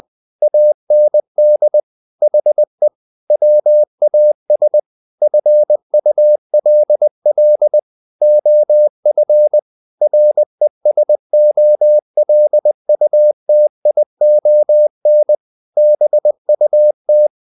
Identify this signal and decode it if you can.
Morse code, heard as a keyed tone, about 20 wpm, 600 Hz. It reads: AND HE WAS FULL OF RESOLUTION BUT
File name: the old man and the sea part 2 20wpm